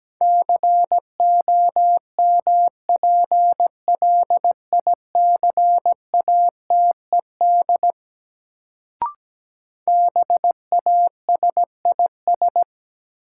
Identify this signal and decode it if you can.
Morse code, heard as a keyed tone, about 17 wpm, 700 Hz. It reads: COMPLICATED BASIS
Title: Sentences from Top 1200 Words - New Word Drill Alt - 17wpm